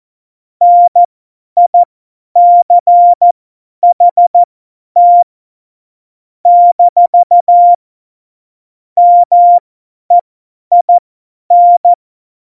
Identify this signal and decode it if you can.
Morse code, heard as a keyed tone, about 14 wpm, spaced as for 10 wpm, 700 Hz. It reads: NICHT - MEIN
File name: Morsekurs / Witze / Witz 27. Oktober 2020, 50 F70